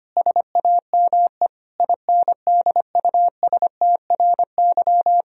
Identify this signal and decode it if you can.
Morse code, heard as a keyed tone, about 25 wpm, 700 Hz. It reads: SAME INDUSTRY
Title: Sentences from Top 1000 Words - Review Part 4 - 25wpm